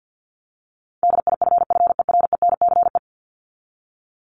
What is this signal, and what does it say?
Morse code, heard as a keyed tone, about 50 wpm, 700 Hz. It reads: DIFFERENCE
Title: ES:S Suffix - Top 100 Words with Base Words 50wpm